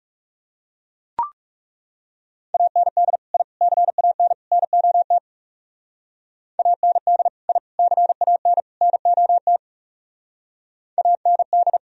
Morse code, heard as a keyed tone, about 37 wpm, 700 Hz. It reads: AND I CAN NOT AND I CAN NOT AND
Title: Speed-Racing - Sets of 4 Words 30wpm